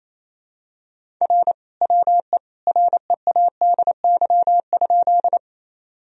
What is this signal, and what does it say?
Morse code, heard as a keyed tone, about 28 wpm, 700 Hz. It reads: R WE READY?
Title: Sentences from Top 400 Words and Abbreviations 28wpm